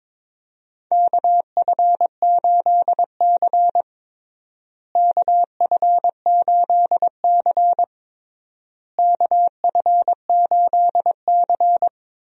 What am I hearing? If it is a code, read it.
Morse code, heard as a keyed tone, about 22 wpm, 700 Hz. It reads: KF8C KF8C KF8C
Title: USA Call Signs - 4 Characters - Repeated 3x - 22wpm